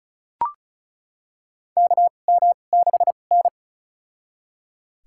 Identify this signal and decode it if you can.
Morse code, heard as a keyed tone, about 34 wpm, spaced as for 25 wpm, 700 Hz. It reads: KM6N
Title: USA Call Signs - 4 Characters - 2x Character Spacing - 35wpm